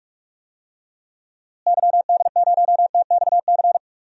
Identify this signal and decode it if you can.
Morse code, heard as a keyed tone, about 46 wpm, 700 Hz. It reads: YD0TXC
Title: DXCC-100 Call Signs - 6 Characters - Repeated 3x - 45wpm